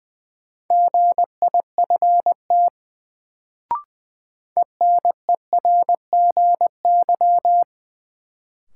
Morse code, heard as a keyed tone, about 20 wpm, 700 Hz. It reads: GIFT ENERGY
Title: Sentences from Top 900 Words - New Word Drill - 20wpm